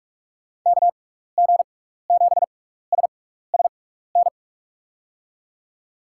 Morse code, heard as a keyed tone, about 44 wpm, spaced as for 15 wpm, 700 Hz. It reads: KG7SSN
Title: USA Call Signs - 6x Character Spacing - 45wpm